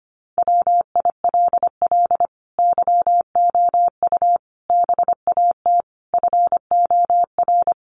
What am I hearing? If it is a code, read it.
Morse code, heard as a keyed tone, about 25 wpm, 700 Hz. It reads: WILL YOU BAT FOR
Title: Sentences from Top 1500 Words 25wpm